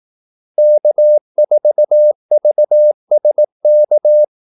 Morse code, heard as a keyed tone, about 18 wpm, 600 Hz. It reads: K4VSK
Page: Kaldesignaler 18 wpm